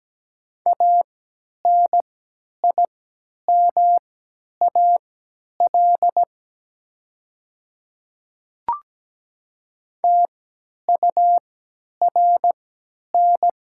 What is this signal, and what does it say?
Morse code, heard as a keyed tone, about 17 wpm, spaced as for 10 wpm, 700 Hz. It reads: ANIMAL TURN
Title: 3x Character Spacing - Top 500 Words - 17wpm